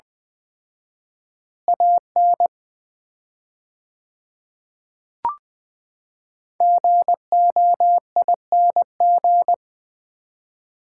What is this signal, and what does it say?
Morse code, heard as a keyed tone, about 21 wpm, 700 Hz. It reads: AN GOING
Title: 4x Word-Spacing - Sets of 2 Words - 20wpm